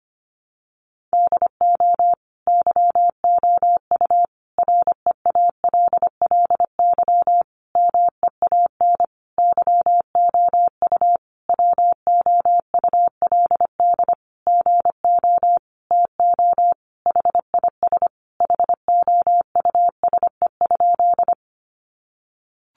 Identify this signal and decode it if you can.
Morse code, heard as a keyed tone, about 25 wpm, 700 Hz. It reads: DO YOU REALLY MEAN YOU WOULD GO TO HIS HOUSE?
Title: Sentences from Top 600 Words 25wpm